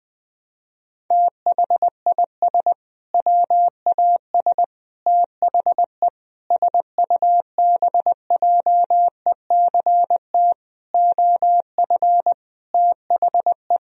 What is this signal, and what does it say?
Morse code, heard as a keyed tone, about 20 wpm, 700 Hz. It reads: THIS WAS THE SUBJECT OF THE
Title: Sentences from Top 1000 Words - Review Part 1 - 20wpm